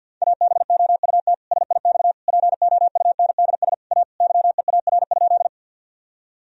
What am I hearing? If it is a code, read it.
Morse code, heard as a keyed tone, about 50 wpm, 700 Hz. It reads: ABOUT SIX POUNDS A YEAR?
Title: Sentences from Top 1000 Words - Review Part 2 - 50wpm